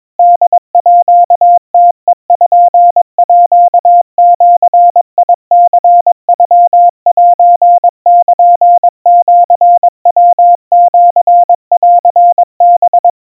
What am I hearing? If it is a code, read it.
Morse code (Japanese Wabun code), heard as a keyed tone, about 22 wpm, 700 Hz. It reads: ホームページニノセルシヤシンハ
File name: kamyu 80 cpm 700 Hz Crispy